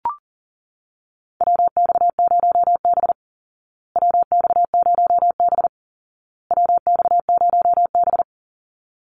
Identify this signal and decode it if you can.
Morse code, heard as a keyed tone, about 40 wpm, 700 Hz. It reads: WX0B WX0B WX0B
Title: Call Signs Repeated 2x - 40wpm